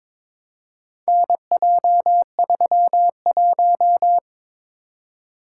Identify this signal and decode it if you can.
Morse code, heard as a keyed tone, about 22 wpm, 700 Hz. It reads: NJ31
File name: practice / 22wpm / Four Letter-Number Combinations - 22wpm